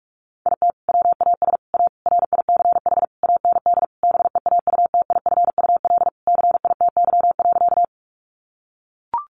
Morse code, heard as a keyed tone, about 44 wpm, 700 Hz. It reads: IT WAS A RICH AND BEAUTIFUL CITY.
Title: Sentences from Top 1100 Words 45wpm